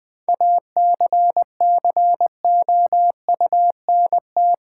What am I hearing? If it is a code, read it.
Morse code, heard as a keyed tone, about 20 wpm, 700 Hz. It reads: ACCOUNT
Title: Sentences from Top 900 Words - New Word Drill - 20wpm